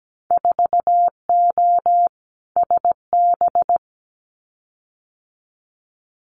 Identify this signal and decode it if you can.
Morse code, heard as a keyed tone, about 17 wpm, 700 Hz. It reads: 4O SB
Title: Winter Field Day Exchanges 17wpm